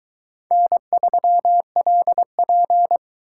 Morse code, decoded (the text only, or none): N3LP